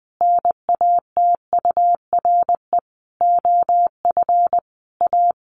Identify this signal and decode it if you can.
Morse code, heard as a keyed tone, about 19 wpm, 700 Hz. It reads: NATURE OF A
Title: Sentences from Top 1400 Words 20wpm